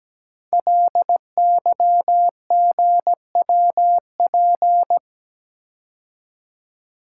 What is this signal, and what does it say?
Morse code, heard as a keyed tone, about 17 wpm, 700 Hz. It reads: LYGWP
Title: Code Groups - 5 Letter:Numbers - 17wpm